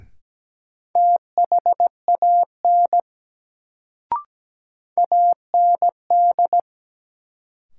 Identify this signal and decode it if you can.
Morse code, heard as a keyed tone, about 17 wpm, 700 Hz. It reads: THAN AND
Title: Sentences from Top 100 Words - New Word Drill - 17wpm